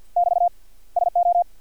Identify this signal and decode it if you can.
Morse code, heard as a keyed tone, about 50 wpm, 700 Hz. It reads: <BT> SO
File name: archived / CRSnet